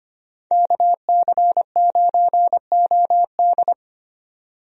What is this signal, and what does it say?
Morse code, heard as a keyed tone, about 25 wpm, 700 Hz. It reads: KC9OD
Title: USA Call Signs - 5 Characters - 25wpm